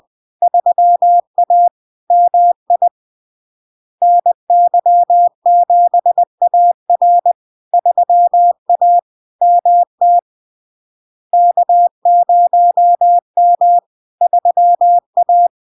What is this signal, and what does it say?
Morse code, heard as a keyed tone, about 20 wpm, 700 Hz. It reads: 3A MI NY7AR 3A MT K0M 3A
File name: LARK-FD-2022-Log-in-CW-20wpm-700-Hz